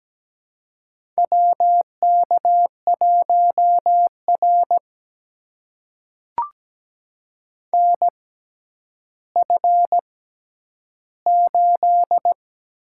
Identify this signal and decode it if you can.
Morse code, heard as a keyed tone, about 17 wpm, spaced as for 10 wpm, 700 Hz. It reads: WK1R NF8
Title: USA Call Signs - 4 Characters - 6x Character Spacing - 17wpm